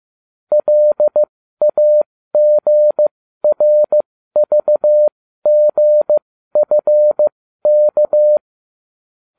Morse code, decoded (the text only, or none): LAGRVGFK